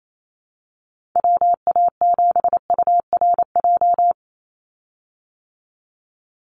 Morse code, heard as a keyed tone, about 28 wpm, 700 Hz. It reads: WA7URJ
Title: USA Call Signs 28wpm